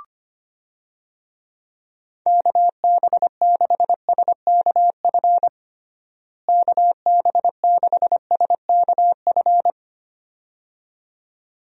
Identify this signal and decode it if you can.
Morse code, heard as a keyed tone, about 25 wpm, 700 Hz. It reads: KB6SKF KB6SKF
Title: USA Call Signs - 6 Characters - Repeated 2x - 25wpm